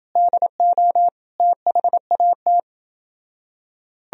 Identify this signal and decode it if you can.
Morse code, heard as a keyed tone, about 29 wpm, 700 Hz. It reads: DO THAT